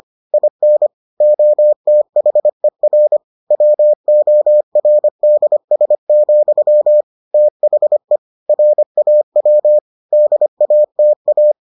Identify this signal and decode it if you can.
Morse code, heard as a keyed tone, about 25 wpm, 600 Hz. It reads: IN OTHER WORDS, THE RAW DATA